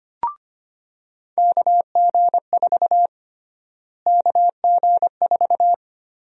Morse code, heard as a keyed tone, about 25 wpm, 700 Hz. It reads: KG4 KG4
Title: USA Call Sign Prefixes - Repeated 2x - 25wpm